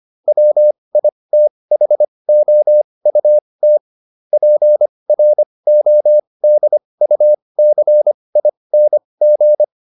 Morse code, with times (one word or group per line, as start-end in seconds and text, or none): 0.3-3.8 WITHOUT
4.3-9.6 PRODUCING